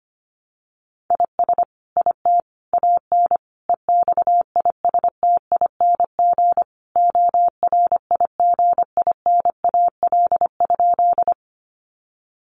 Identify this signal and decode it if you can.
Morse code, heard as a keyed tone, about 25 wpm, 700 Hz. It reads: IS IT AN EXISTING ORIGINAL?
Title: Sentences from Top 1500 Words 25wpm